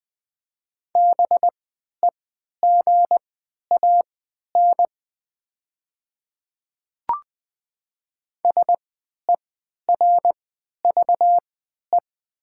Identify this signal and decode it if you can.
Morse code, heard as a keyed tone, about 20 wpm, spaced as for 11 wpm, 700 Hz. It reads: BEGAN SERVE
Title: 3x Character Spacing - Top 500 Words - 20wpm